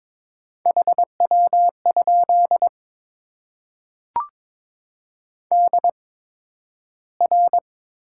Morse code, heard as a keyed tone, about 22 wpm, spaced as for 12 wpm, 700 Hz. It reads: HW? DR
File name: Top 90 QSO Elements - 8x Character Spacing - 22wpm